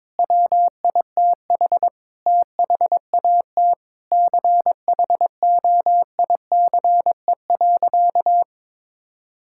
Morse code, heard as a keyed tone, about 22 wpm, 700 Hz. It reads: WITH THAT CHOICE.
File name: Sentences with 7-Word Phrases - 22wpm